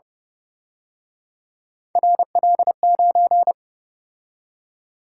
Morse code, heard as a keyed tone, about 29 wpm, 700 Hz. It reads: RL9